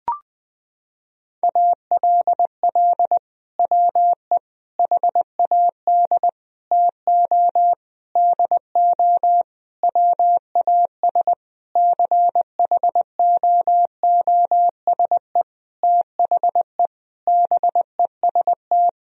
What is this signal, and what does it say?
Morse code, heard as a keyed tone, about 21 wpm, 700 Hz. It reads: ALL WE HAD TO DO WAS CHOOSE THE BEST
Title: Sentences from Top 1000 Words - Review Part 4 - 20wpm